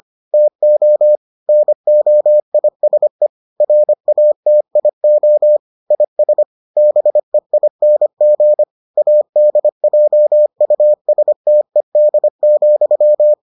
Morse code, heard as a keyed tone, about 25 wpm, 600 Hz. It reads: TO NOISE RATIO IS BEING ADJUSTED,